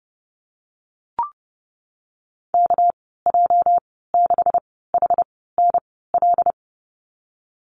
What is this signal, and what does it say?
Morse code, heard as a keyed tone, about 30 wpm, spaced as for 17 wpm, 700 Hz. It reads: KJ6HNL
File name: Lesson 37g - Letter x - Call Signs 30wpm